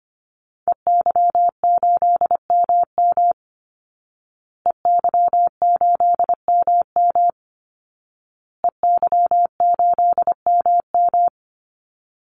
Morse code, heard as a keyed tone, about 25 wpm, 700 Hz. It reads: EY8MM EY8MM EY8MM
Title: Call Signs Repeated 2x - 25wpm